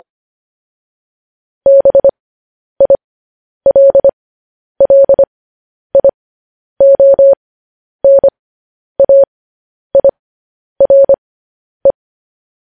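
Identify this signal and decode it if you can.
Morse code, heard as a keyed tone, about 25 wpm, spaced as for 10 wpm, 550 Hz. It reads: BILLIONAIRE
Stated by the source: Morse code news headlines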